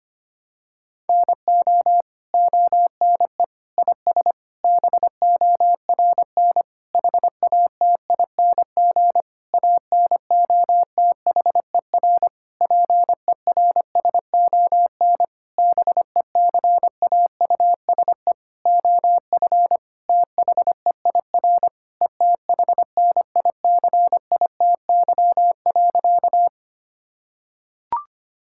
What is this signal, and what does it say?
Morse code, heard as a keyed tone, about 25 wpm, 700 Hz. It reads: NO ONE IS BORN HATING ANOTHER PERSON BECAUSE OF THEIR ETHNICITY.
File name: practice / 25wpm / Sentences with 9-Word Phrases - 25wpm